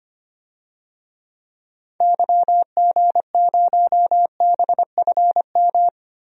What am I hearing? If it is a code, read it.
Morse code, heard as a keyed tone, about 25 wpm, 700 Hz. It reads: YG0BFM
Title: DXCC-100 Call Signs - 6 Characters - Repeated 2x - 25wpm